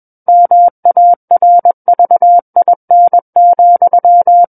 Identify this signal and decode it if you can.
Morse code, heard as a keyed tone, about 21 wpm, 700 Hz. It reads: MARVIN,
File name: Book One / 12-Chapter 12 - 21 WPM